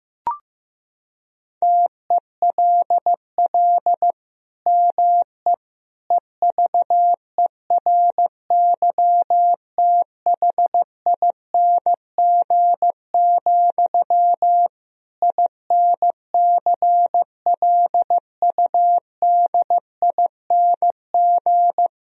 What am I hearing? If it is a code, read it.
Morse code, heard as a keyed tone, about 15 wpm, 700 Hz. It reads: TELL ME EVERYTHING, INCLUDING
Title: Sentences from Top 1000 Words - Review Part 4 - 15wpm